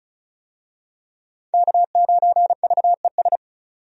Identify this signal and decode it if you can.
Morse code, heard as a keyed tone, about 36 wpm, 700 Hz. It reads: K9VES
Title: USA Call Signs - 5 Characters - 35wpm